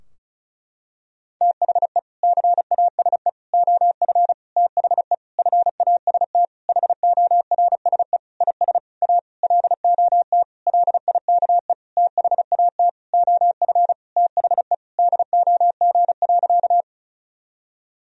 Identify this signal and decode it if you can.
Morse code, heard as a keyed tone, about 34 wpm, 700 Hz. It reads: THE CASE OF THE FAST HORSE IS A LOT LIKE THAT OF THE DOG.
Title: Sentences from Top 1000 Words - Review Part 2 - 35wpm